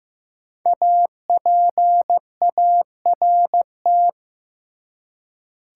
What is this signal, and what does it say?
Morse code, heard as a keyed tone, about 15 wpm, 700 Hz. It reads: APART
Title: Five Letter Words - 15wpm